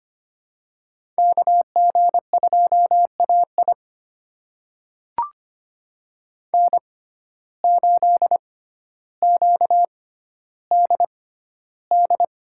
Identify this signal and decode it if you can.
Morse code, heard as a keyed tone, about 25 wpm, spaced as for 13 wpm, 700 Hz. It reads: KG2AI N8QDD
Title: USA Call Signs - 5 Characters - 6x Character Spacing - 25wpm